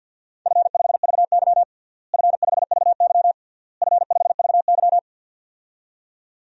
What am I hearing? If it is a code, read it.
Morse code, heard as a keyed tone, about 52 wpm, 700 Hz. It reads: F5VY F5VY F5VY
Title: DXCC-100 Call Signs - 4 Characters - Repeated 3x - 50wpm